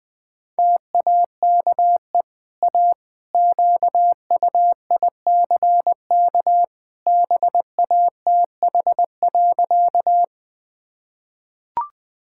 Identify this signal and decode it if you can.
Morse code, heard as a keyed tone, about 20 wpm, 700 Hz. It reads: TAKE A QUICK BATH.
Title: Easy Sentences with Hints - 20wpm